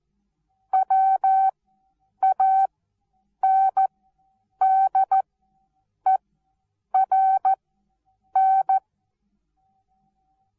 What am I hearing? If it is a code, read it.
Morse code, heard as a keyed tone, about 14 wpm, spaced as for 8 wpm, 770 Hz. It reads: WANDERN
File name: Onlinelehrgang / morsen / DJ4UF Morsen19